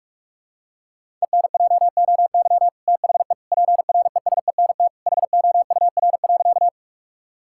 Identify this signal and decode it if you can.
Morse code, heard as a keyed tone, about 46 wpm, 700 Hz. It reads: ENJOY THE PRESENT HOUR.
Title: Sentences from Top 1000 Words - Review Part 5 - 45wpm